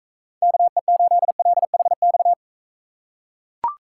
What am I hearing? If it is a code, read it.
Morse code, heard as a keyed tone, about 40 wpm, 700 Hz. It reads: KE8LHX